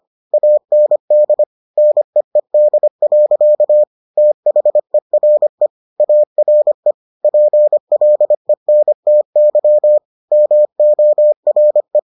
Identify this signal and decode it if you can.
Morse code, heard as a keyed tone, about 25 wpm, 600 Hz. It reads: AND NEED. THERE ARE PLENTY MORE